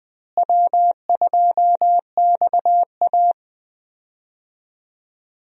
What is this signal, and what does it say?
Morse code, heard as a keyed tone, about 20 wpm, 700 Hz. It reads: W2XA
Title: USA Call Signs - 4 Characters - Repeated 2x - 20wpm